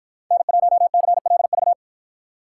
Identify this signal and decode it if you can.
Morse code, heard as a keyed tone, about 53 wpm, 700 Hz. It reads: N1CLV